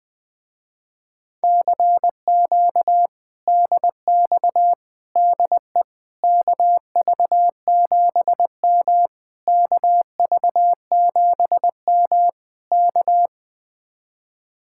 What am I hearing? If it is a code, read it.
Morse code, heard as a keyed tone, about 20 wpm, 700 Hz. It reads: CQ DX DE KV7M KV7M K
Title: Speed-Racing Common QSO Phrases - 20wpm